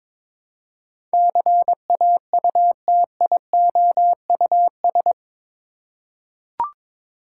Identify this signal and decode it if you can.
Morse code, heard as a keyed tone, about 22 wpm, 700 Hz. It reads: CAUTIOUS
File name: practice / 22wpm / Eight Letter Words - 22wpm